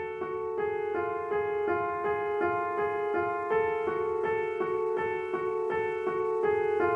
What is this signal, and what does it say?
There is no code.